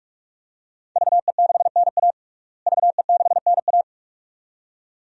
Morse code, heard as a keyed tone, about 41 wpm, 700 Hz. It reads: VE6NA VE6NA
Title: DXCC-100 Call Signs - 5 Characters - Repeated 2x - 45wpm